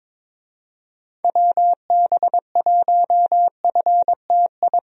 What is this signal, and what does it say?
Morse code, heard as a keyed tone, about 22 wpm, 700 Hz. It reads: WB1FTI
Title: USA Call Signs - 6 Characters - 22wpm